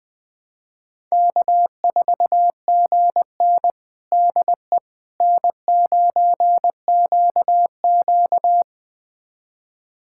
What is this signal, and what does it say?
Morse code, heard as a keyed tone, about 20 wpm, 700 Hz. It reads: K4GN DE N9QQ
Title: USA Callsign DE Callsign - 4 Characters - 20wpm